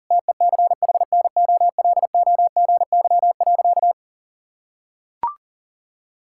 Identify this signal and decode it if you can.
Morse code, heard as a keyed tone, about 40 wpm, 700 Hz. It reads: TECHNOLOGY.
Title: Sentences from Top 1000 Words - Review Part 5 - 40wpm